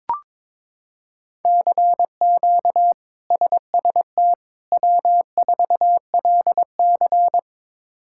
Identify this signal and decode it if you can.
Morse code, heard as a keyed tone, about 22 wpm, 700 Hz. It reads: CQ SST W4LC
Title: K1USN SST QSO Practice - 22wpm